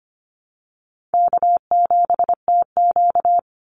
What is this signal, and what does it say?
Morse code, heard as a keyed tone, about 25 wpm, 700 Hz. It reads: K7TQ
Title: NA SOTA Chaser Callsigns 25wpm